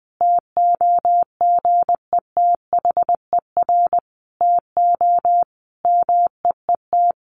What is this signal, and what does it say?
Morse code, heard as a keyed tone, about 20 wpm, 700 Hz. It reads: TOGETHER TO MEET
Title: Sentences from Top 1100 Words 20wpm